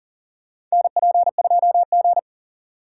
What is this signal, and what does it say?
Morse code, heard as a keyed tone, about 40 wpm, 700 Hz. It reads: NP2G
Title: USA Call Signs - 4 Characters - 6x Character Spacing - 40wpm